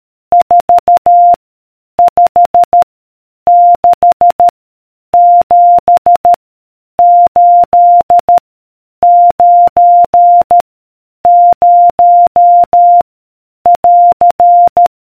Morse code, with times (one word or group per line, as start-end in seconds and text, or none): 0.3-1.3 4
2.0-2.8 5
3.5-4.5 6
5.1-6.3 7
7.0-8.4 8
9.0-10.6 9
11.3-13.0 0
13.7-14.9 <AR>